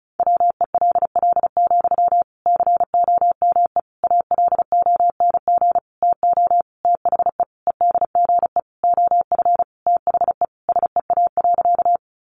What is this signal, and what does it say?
Morse code, heard as a keyed tone, about 34 wpm, 700 Hz. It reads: WELL, COME ALONG TO THE EDGE OF THE SEA.
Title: Sentences from Top 1300 Words 35wpm